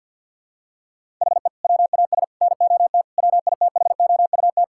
Morse code, heard as a keyed tone, about 53 wpm, 700 Hz. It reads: HE WAS NOT WITHOUT
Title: Sentences from Top 1000 Words - Review Part 6 - 50wpm